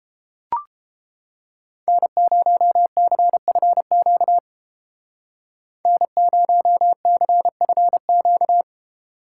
Morse code, heard as a keyed tone, about 30 wpm, 700 Hz. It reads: N0CFQ N0CFQ